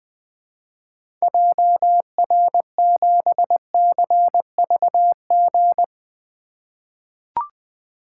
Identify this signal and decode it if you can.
Morse code, heard as a keyed tone, about 20 wpm, 700 Hz. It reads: JR7CVG
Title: 2023 POTA Activator Call Signs - 20wpm